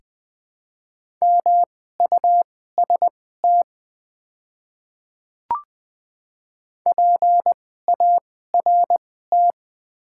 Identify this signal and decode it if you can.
Morse code, heard as a keyed tone, about 20 wpm, spaced as for 14 wpm, 700 Hz. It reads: MUST PART
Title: 2x Character Spacing - Top 200 Words - 20wpm